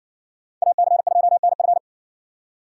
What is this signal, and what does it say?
Morse code, heard as a keyed tone, about 62 wpm, 700 Hz. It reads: AC2NF
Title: Speed Racing - USA 2x2 Call Signs 50wpm